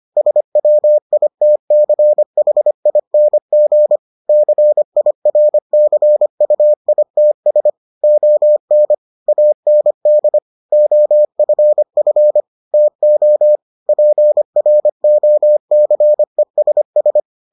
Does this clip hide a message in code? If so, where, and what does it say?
Yes: Morse code from 0.2 to 17.2 s, heard as a keyed tone, about 25 wpm, 600 Hz. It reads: SWITCHING CIRCUITS ON AND OFF TO PROCESS